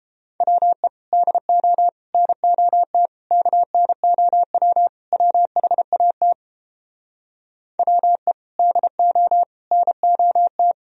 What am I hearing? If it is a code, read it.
Morse code, heard as a keyed tone, about 29 wpm, 700 Hz. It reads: WE DO NOT KNOW WHAT WE DO NOT